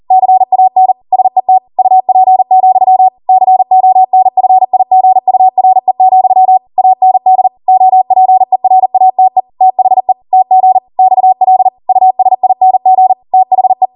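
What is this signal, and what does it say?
Morse code, heard as a keyed tone, about 40 wpm, 750 Hz. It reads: CAN SET UP, CONFIGURE, AND OPERATE THE TG XL USING THE
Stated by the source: Code practice at 40 wpm.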